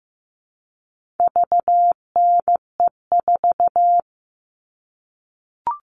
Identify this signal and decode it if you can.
Morse code, heard as a keyed tone, about 15 wpm, 700 Hz. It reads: VNE4
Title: All Cut Numbers - 4-Digit 15wpm